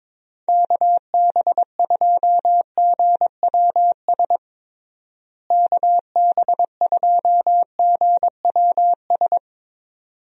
Morse code, heard as a keyed tone, about 22 wpm, 700 Hz. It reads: KB2GWS KB2GWS
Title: USA Call Signs - 6 Characters - Repeated 2x - 22wpm